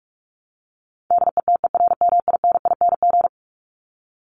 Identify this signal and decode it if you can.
Morse code, heard as a keyed tone, about 44 wpm, 700 Hz. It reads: DETERMINING
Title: ING Suffix - Top 100 Words with Base Words 45wpm